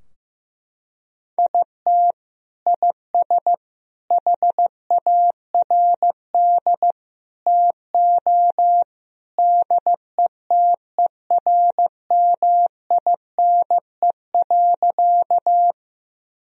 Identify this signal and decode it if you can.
Morse code, heard as a keyed tone, about 16 wpm, 700 Hz. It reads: IT IS HARD TO DETERMINE.
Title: Sentences from Top 1000 Words - Review Part 4 - 15wpm